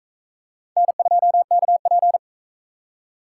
Morse code, heard as a keyed tone, about 41 wpm, 700 Hz. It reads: N2KP